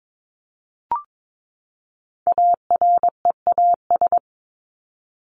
Morse code, heard as a keyed tone, about 22 wpm, 700 Hz. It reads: AREAS
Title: ES:S Suffix - Top 100 Words with Base Words 22wpm